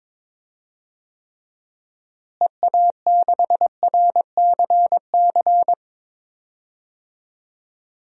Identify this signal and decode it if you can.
Morse code, heard as a keyed tone, about 22 wpm, 700 Hz. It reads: EA6RCC
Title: DXCC-100 Call Signs - 6 Characters - Repeated 3x - 22wpm